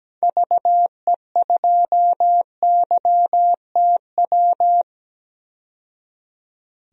Morse code, heard as a keyed tone, about 17 wpm, 700 Hz. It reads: VE2YTW
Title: DXCC-100 Call Signs - 6 Characters - 17wpm